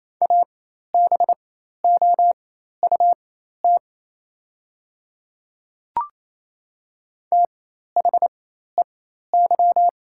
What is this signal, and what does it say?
Morse code, heard as a keyed tone, about 28 wpm, spaced as for 13 wpm, 700 Hz. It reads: ABOUT THEY
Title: 4x Character Spacing - Top 200 Words - 28wpm